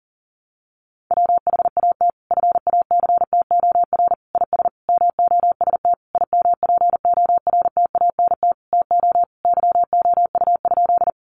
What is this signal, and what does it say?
Morse code, heard as a keyed tone, about 40 wpm, 700 Hz. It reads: WHAT FACTOR IS MOST IMPORTANT TO YOU?
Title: Sentences from Top 1300 Words 40wpm